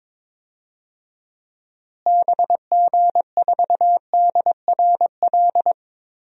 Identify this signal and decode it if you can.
Morse code, heard as a keyed tone, about 21 wpm, 700 Hz. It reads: BG4DRL
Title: DXCC-100 Call Signs - 6 Characters - Repeated 3x - 22wpm